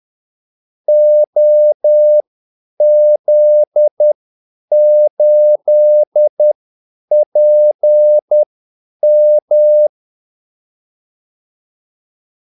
Morse code, heard as a keyed tone, about 10 wpm, spaced as for 8 wpm, 600 Hz. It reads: OZ8PM
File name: Callsigns-10-8wpm